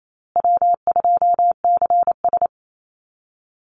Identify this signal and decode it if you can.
Morse code, heard as a keyed tone, about 28 wpm, 700 Hz. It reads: W2CS
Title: Easy Call Signs Repeated 3x - 28wpm